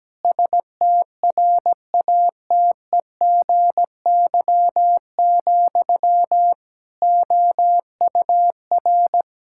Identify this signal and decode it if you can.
Morse code, heard as a keyed tone, about 17 wpm, 700 Hz. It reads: STRATEGY, OUR
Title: Sentences from Top 1000 Words - Review Part 5 - 17wpm